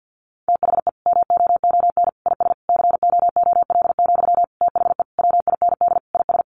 Morse code, heard as a keyed tone, about 50 wpm, 700 Hz. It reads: THE MOON IS COOL, THE WIND IS
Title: Sentences from Top 1200 Words 50wpm